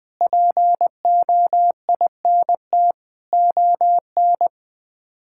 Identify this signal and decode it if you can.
Morse code, heard as a keyed tone, about 21 wpm, 700 Hz. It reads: POINT ON
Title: Sets of 8 Words - 20wpm